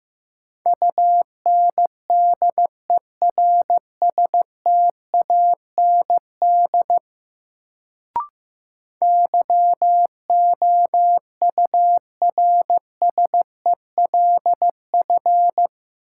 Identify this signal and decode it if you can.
Morse code, heard as a keyed tone, about 16 wpm, 700 Hz. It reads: UNDERSTAND YOURSELF
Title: Sentences from Top 600 Words - New Word Drill - 15wpm